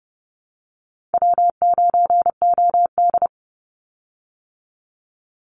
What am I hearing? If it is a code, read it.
Morse code, heard as a keyed tone, about 30 wpm, 700 Hz. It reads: W9OD
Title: Easy Call Signs Repeated 2x - 30wpm